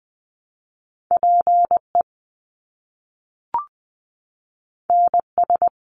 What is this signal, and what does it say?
Morse code, heard as a keyed tone, about 19 wpm, 700 Hz. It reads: PE NS
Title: Canadian Provinces and Territories 20wpm